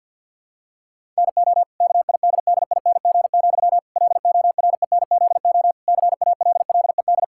Morse code, heard as a keyed tone, about 50 wpm, 700 Hz. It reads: NO KIDDING, LORENZO CALLED
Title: Pangram Challenge v2 - 50wpm